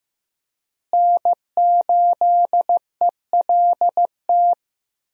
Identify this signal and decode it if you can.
Morse code, heard as a keyed tone, about 14 wpm, 700 Hz. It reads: N8ELT
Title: USA Call Signs - 5 Characters - Repeated 3x - 15wpm